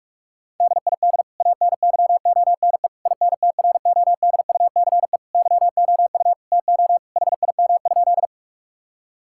Morse code, heard as a keyed tone, about 46 wpm, 700 Hz. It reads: DID ANYONE INTRODUCE YOU TO HIM?
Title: Sentences from Top 1600 Words 45wpm